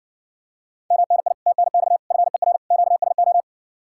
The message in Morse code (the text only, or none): MNI TNX FER QSO